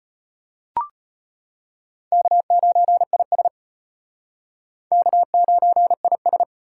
Morse code, heard as a keyed tone, about 34 wpm, 700 Hz. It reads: K9IS K9IS